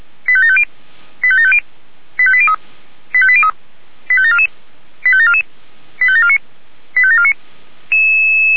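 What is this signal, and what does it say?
Five-tone selective call (ZVEI), heard as quick runs of five tones, tones of about 70 ms. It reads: 85750 85750 85803 85803 85744 85744 85749 85749
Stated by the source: ZVEI five-tone selective call 85730 received on an SDR.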